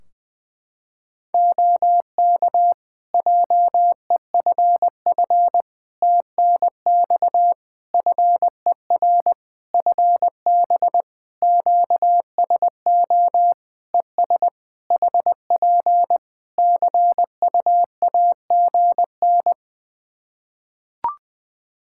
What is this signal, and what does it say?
Morse code, heard as a keyed tone, about 20 wpm, 700 Hz. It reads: OK JEFF TNX FER FB QSO ES HP CUAGN
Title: Practice QSO - Exchange 4 - Ending - 20wpm